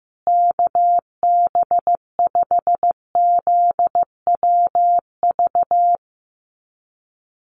Morse code, heard as a keyed tone, about 15 wpm, 700 Hz. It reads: KB5ZWV
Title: USA Call Signs Repeated 3x 15wpm